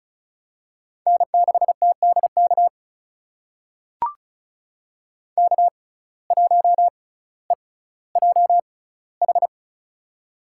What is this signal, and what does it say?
Morse code, heard as a keyed tone, about 36 wpm, spaced as for 18 wpm, 700 Hz. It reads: N6TDK K1EJH
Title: USA Call Signs - 5 Characters - 6x Character Spacing - 35wpm